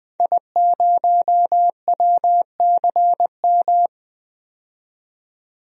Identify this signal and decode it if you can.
Morse code, heard as a keyed tone, about 21 wpm, 700 Hz. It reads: I0WCM
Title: DXCC-100 Call Signs - 5 Characters - 20wpm